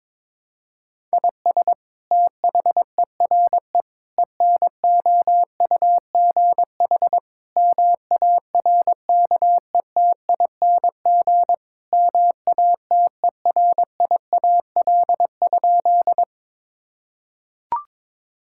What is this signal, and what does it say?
Morse code, heard as a keyed tone, about 21 wpm, 700 Hz. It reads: IS THERE ENOUGH MARKETING MATERIAL?
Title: Sentences from Top 1000 Words - Review Part 5 - 22wpm